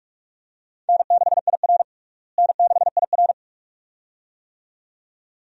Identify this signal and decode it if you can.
Morse code, heard as a keyed tone, about 47 wpm, 700 Hz. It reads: N6IR N6IR
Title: USA Call Signs - 4 Characters - Repeated 2x - 45wpm